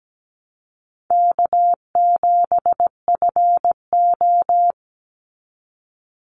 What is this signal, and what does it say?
Morse code, heard as a keyed tone, about 17 wpm, 700 Hz. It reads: K7FO
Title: NA SOTA Chaser Callsigns 17wpm